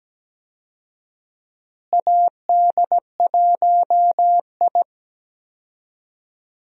Morse code, heard as a keyed tone, about 17 wpm, 700 Hz. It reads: AD1I
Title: USA Call Signs - 4 Characters - Repeated 2x - 17wpm